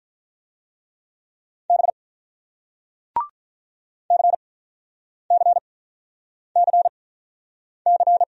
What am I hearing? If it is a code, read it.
Morse code, heard as a keyed tone, about 46 wpm, spaced as for 9 wpm, 700 Hz. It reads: B CCCC